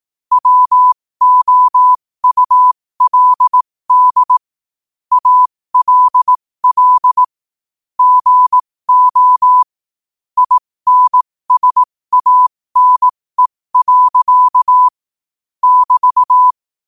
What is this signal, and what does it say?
Morse code, heard as a keyed tone, about 18 wpm, spaced as for 15 wpm, 1000 Hz. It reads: WOULD ALL GO INSANE. <BT>
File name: qotdfiles / 15wpm